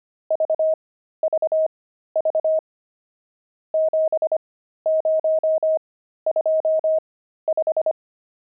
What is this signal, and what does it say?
Morse code, heard as a keyed tone, about 25 wpm, spaced as for 13 wpm, 630 Hz. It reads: VVV 7025
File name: pangram-fs-13-cs-25-nums10